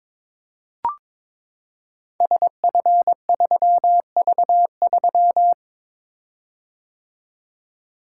Morse code, heard as a keyed tone, about 23 wpm, 700 Hz. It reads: SF3V3
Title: Code Groups - 5 Letter:Numbers - 22wpm